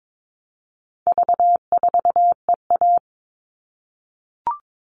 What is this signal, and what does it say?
Morse code, heard as a keyed tone, about 22 wpm, 700 Hz. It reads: V4EA
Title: All Cut Numbers - 4-Digit 22wpm